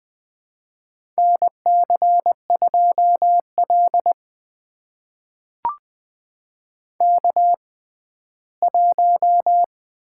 Morse code, heard as a keyed tone, about 20 wpm, spaced as for 14 wpm, 700 Hz. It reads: NC2L K1
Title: USA Call Signs - 4 Characters - 6x Character Spacing - 20wpm